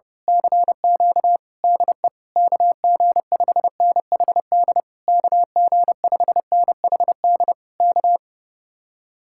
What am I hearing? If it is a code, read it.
Morse code, heard as a keyed tone, about 29 wpm, 700 Hz. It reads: CQ DE KG5NHD KG5NHD K
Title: Speed-Racing Common QSO Phrases Encore - 30wpm